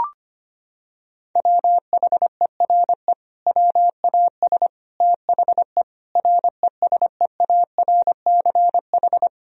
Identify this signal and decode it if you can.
Morse code, heard as a keyed tone, about 25 wpm, 700 Hz. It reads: WHERE WAS THE RESEARCH
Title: Sentences from Top 1000 Words - Review Part 4 - 25wpm